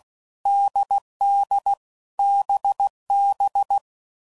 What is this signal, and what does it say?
Morse code, heard as a keyed tone, about 16 wpm, 800 Hz. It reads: DD BB
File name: Combinations / Difficult Combos / NDB NDB6